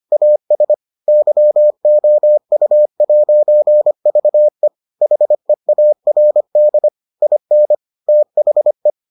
Morse code, heard as a keyed tone, about 25 wpm, 600 Hz. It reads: AS YOU'VE HEARD IN THE